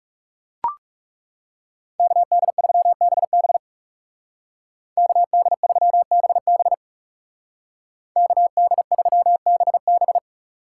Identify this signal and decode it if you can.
Morse code, heard as a keyed tone, about 35 wpm, 700 Hz. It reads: KD3BB KD3BB KD3BB
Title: Speed Racing - USA 2x2 Call Signs 30wpm